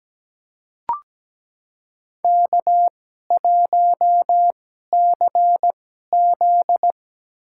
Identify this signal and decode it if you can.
Morse code, heard as a keyed tone, about 17 wpm, spaced as for 12 wpm, 700 Hz. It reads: K1CZ
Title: USA Call Signs - 4 Characters - 2x Character Spacing - 17wpm